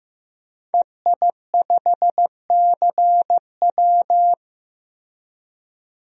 Morse code, heard as a keyed tone, about 15 wpm, 700 Hz. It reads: EI5CW
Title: DXCC-100 Call Signs - 5 Characters - Repeated 2x - 15wpm